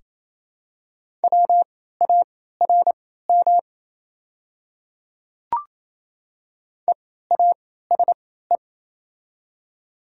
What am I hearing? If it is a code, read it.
Morse code, heard as a keyed tone, about 28 wpm, spaced as for 16 wpm, 700 Hz. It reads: WARM EASE
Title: 3x Character Spacing - Top 500 Words - 28wpm